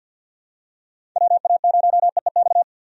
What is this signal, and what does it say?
Morse code, heard as a keyed tone, about 50 wpm, 700 Hz. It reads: WA0EEX
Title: USA Call Signs - 6 Characters - 50wpm